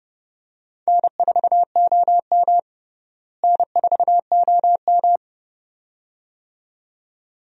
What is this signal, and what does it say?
Morse code, heard as a keyed tone, about 31 wpm, 700 Hz. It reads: N4OM N4OM
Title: USA Call Signs - 4 Characters - Repeated 2x - 30wpm